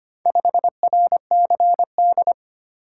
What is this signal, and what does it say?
Morse code, heard as a keyed tone, about 25 wpm, 700 Hz. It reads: 5RCD
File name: practice / 25wpm / Four Letter-Number Combinations - 25wpm